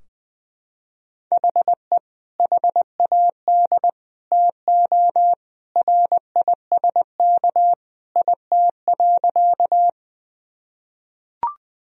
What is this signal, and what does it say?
Morse code, heard as a keyed tone, about 21 wpm, 700 Hz. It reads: HE HAD TO RISK IT.
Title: Sentences from Top 1000 Words - Review Part 4 - 20wpm